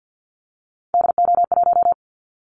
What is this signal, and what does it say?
Morse code, heard as a keyed tone, about 50 wpm, 700 Hz. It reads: DO1
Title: Callsign Prefixes for Top 25 DXCC - 50wpm